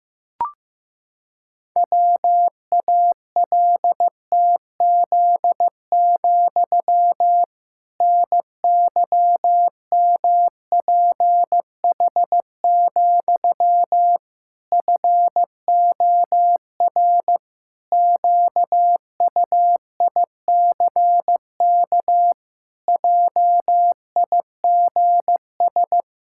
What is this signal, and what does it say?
Morse code, heard as a keyed tone, about 16 wpm, 700 Hz. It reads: WALTZ, NYMPH, FOR QUICK JIGS
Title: Pangram Challenge v2 - 15wpm